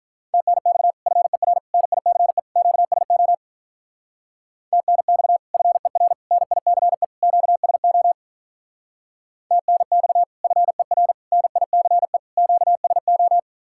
Morse code, heard as a keyed tone, about 46 wpm, 700 Hz. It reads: TNX FER NICE QSO TNX FER NICE QSO TNX FER NICE QSO